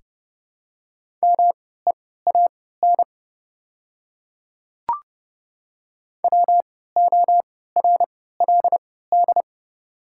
Morse code, heard as a keyed tone, about 29 wpm, spaced as for 17 wpm, 700 Hz. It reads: MEAN WORLD
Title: 3x Character Spacing - Top 200 Words - 30wpm